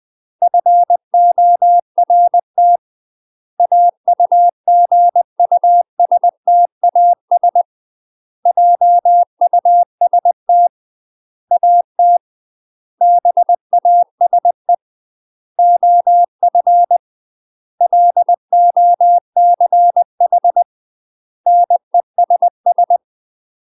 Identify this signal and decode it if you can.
Morse code, heard as a keyed tone, about 20 wpm, 700 Hz. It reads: FORT AUGUSTAS JUST AT BASE OF LOCH NESS
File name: UK MASTER QSOs TRAINING / 20WPM / UK MASTER QSOs 0051